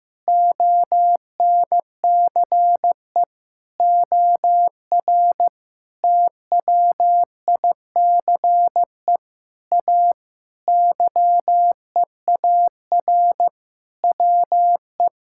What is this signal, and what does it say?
Morse code, heard as a keyed tone, about 15 wpm, 700 Hz. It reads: ONCE OR TWICE A YEAR WE
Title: Sentences from Top 1600 Words 15wpm